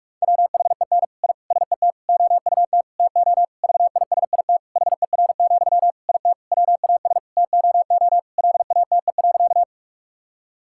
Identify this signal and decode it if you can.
Morse code, heard as a keyed tone, about 46 wpm, 700 Hz. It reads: WHEN I SET OUT TO VISIT HER, IT WAS TOO LATE.
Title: Sentences from Top 1000 Words - Review Part 6 - 45wpm